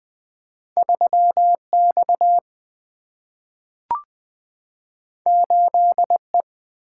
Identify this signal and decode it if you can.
Morse code, heard as a keyed tone, about 20 wpm, 700 Hz. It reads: 3 X 8 E
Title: Two Letter-Number Combinations - 20wpm